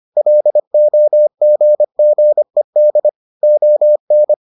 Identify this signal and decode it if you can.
Morse code, heard as a keyed tone, about 25 wpm, 600 Hz. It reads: LOGGED ON